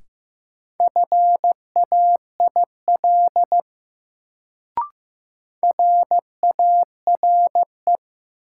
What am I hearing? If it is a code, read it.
Morse code, heard as a keyed tone, about 15 wpm, 700 Hz. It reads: FAIL RARE
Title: Sentences from Top 1100 Words - New Word Drill - 15wpm